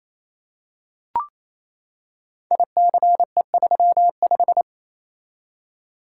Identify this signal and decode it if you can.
Morse code, heard as a keyed tone, about 27 wpm, 700 Hz. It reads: ICE35
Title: Code Groups - 5 Letter:Numbers - 28wpm